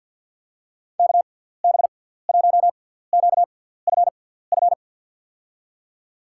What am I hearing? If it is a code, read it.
Morse code, heard as a keyed tone, about 48 wpm, spaced as for 17 wpm, 700 Hz. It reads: KB1QFF
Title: USA Call Signs - 6x Character Spacing - 50wpm